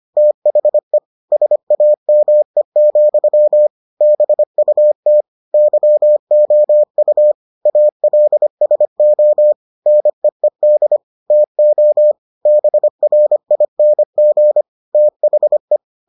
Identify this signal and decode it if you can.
Morse code, heard as a keyed tone, about 25 wpm, 600 Hz. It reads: THE SAME, BUT YOU ALSO NEED TO BRING THE